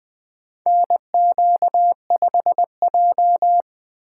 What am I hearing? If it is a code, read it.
Morse code, heard as a keyed tone, about 20 wpm, 700 Hz. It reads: NQ5J
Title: USA Call Signs - 4 Characters - 3x Character Spacing - 20wpm